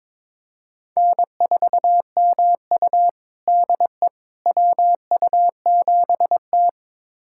Morse code, heard as a keyed tone, about 21 wpm, 700 Hz. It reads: N4MU DE WU7T
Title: USA Callsign DE Callsign - 4 Characters - 22wpm